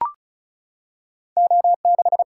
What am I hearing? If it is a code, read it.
Morse code, heard as a keyed tone, about 36 wpm, 700 Hz. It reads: O6
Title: Single-Letter Single-Number 35wpm